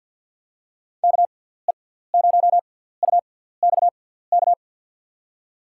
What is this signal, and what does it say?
Morse code, heard as a keyed tone, about 49 wpm, spaced as for 17 wpm, 700 Hz. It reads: KE0UXK
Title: USA Call Signs - 6x Character Spacing - 50wpm